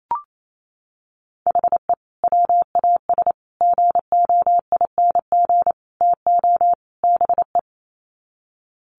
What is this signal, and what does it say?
Morse code, heard as a keyed tone, about 28 wpm, 700 Hz. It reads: HE WAS GOING TO BE
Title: Sets of 5 Words 28wpm